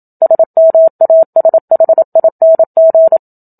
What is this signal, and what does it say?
Morse code, heard as a keyed tone, about 27 wpm, 650 Hz. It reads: SMASHING
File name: Book Two / 01-Chapter 1 - 27 WPM